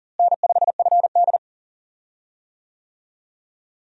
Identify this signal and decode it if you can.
Morse code, heard as a keyed tone, about 40 wpm, 700 Hz. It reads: N5FD
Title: USA Call Signs - 4 Characters - Repeated 2x - 40wpm